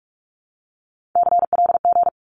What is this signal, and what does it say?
Morse code, heard as a keyed tone, about 46 wpm, 700 Hz. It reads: CLG
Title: Abbreviations 45wpm